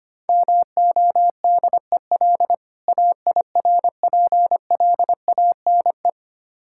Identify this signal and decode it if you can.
Morse code, heard as a keyed tone, about 25 wpm, 700 Hz. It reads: MODEL AIRPLANE
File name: practice / 25wpm / Difficult Words with Hints - 25wpm